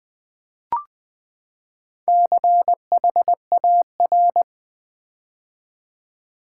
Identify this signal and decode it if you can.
Morse code, heard as a keyed tone, about 20 wpm, 700 Hz. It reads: CHAR
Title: Build A Word - 4 Letters At A Time - 12-Letter Words - 20wpm